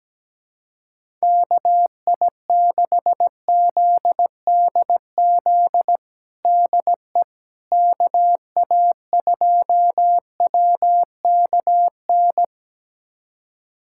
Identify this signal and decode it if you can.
Morse code, heard as a keyed tone, about 17 wpm, 700 Hz. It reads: KI6ZDZ DE KA2WKN
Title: USA Callsign DE Callsign 17wpm